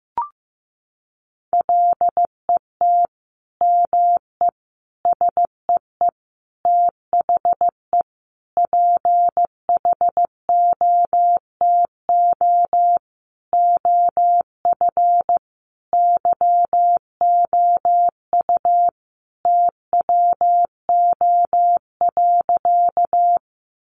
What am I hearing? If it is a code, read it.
Morse code, heard as a keyed tone, about 15 wpm, 700 Hz. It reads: LET ME SEE THE PHOTO OF YOU TWO.
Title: Sentences from Top 1200 Words 15wpm